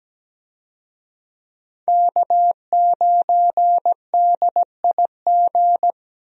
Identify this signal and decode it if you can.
Morse code, heard as a keyed tone, about 17 wpm, 700 Hz. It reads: K9DIG
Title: USA Call Signs - 5 Characters - Repeated 3x - 17wpm